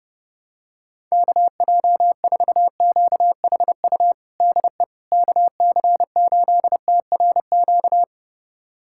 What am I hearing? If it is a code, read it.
Morse code, heard as a keyed tone, about 30 wpm, 700 Hz. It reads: KJ4QHU DE KC8TRQ
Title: USA Callsign DE Callsign 30wpm